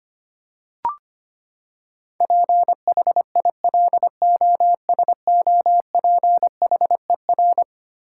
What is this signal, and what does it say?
Morse code, heard as a keyed tone, about 25 wpm, 700 Hz. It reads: PHILOSOPHER
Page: morse-code-ninja-practice-25wpm